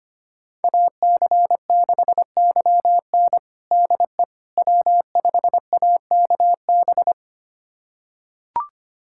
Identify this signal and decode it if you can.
Morse code, heard as a keyed tone, about 25 wpm, 700 Hz. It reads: AC6YN DE W5AKB
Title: USA Callsign DE Callsign - 5 Characters - 25wpm